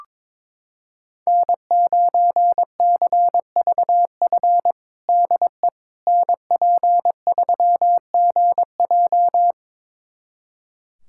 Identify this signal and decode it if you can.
Morse code, heard as a keyed tone, about 21 wpm, 700 Hz. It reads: N9CVF DE NP3GJ
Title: USA Callsign DE Callsign - 5 Characters - 22wpm